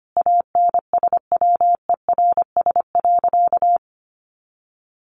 Morse code, heard as a keyed tone, about 25 wpm, 700 Hz. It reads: ANSWERS.
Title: Additional Easy Phrases 25wpm